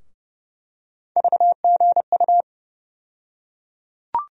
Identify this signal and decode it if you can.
Morse code, heard as a keyed tone, about 30 wpm, 700 Hz. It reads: VGU
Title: USA Call Sign Postfixes - 3 Letters 30wpm